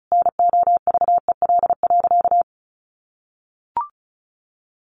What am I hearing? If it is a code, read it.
Morse code, heard as a keyed tone, about 34 wpm, 700 Hz. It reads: NOVEL.
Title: Sentences from Top 1300 Words 35wpm